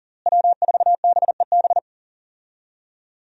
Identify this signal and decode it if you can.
Morse code, heard as a keyed tone, about 40 wpm, 700 Hz. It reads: W4BEB
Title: USA Call Signs - 5 Characters - Repeated 2x - 40wpm